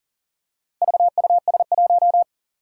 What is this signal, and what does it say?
Morse code, heard as a keyed tone, about 38 wpm, 700 Hz. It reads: VUS1
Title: Four Challenging Letter-Number Combinations 40wpm